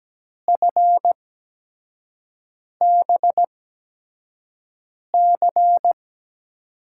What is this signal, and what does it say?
Morse code, heard as a keyed tone, about 17 wpm, spaced as for 5 wpm, 700 Hz. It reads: FBC
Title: USA Call Signs - 5 Characters - 8x Character Spacing - 17wpm